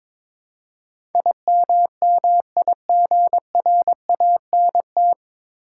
Morse code, heard as a keyed tone, about 22 wpm, 700 Hz. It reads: IMMIGRANT
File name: 9 Letter Words - First 3 Letters Given - 22wpm